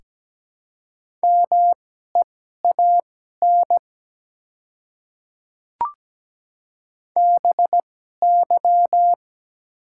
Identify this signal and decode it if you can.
Morse code, heard as a keyed tone, about 17 wpm, spaced as for 12 wpm, 700 Hz. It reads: MEAN BY
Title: 2x Character Spacing - Top 200 Words - 17wpm